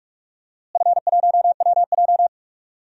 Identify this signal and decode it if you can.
Morse code, heard as a keyed tone, about 46 wpm, 700 Hz. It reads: F1WJ
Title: DXCC-100 Call Signs - 4 Characters - Repeated 3x - 45wpm